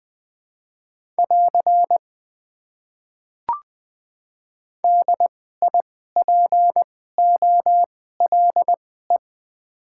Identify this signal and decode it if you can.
Morse code, heard as a keyed tone, about 21 wpm, spaced as for 15 wpm, 700 Hz. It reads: <AR> DIPOLE
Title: Top 90 QSO Elements - 2x Character Spacing - 20wpm